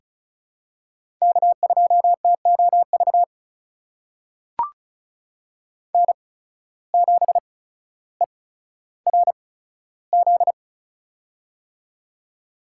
Morse code, heard as a keyed tone, about 35 wpm, spaced as for 15 wpm, 690 Hz. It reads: K2TOV N7ERZ
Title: USA Call Signs - 5 Characters - 8x Character Spacing - 35wpm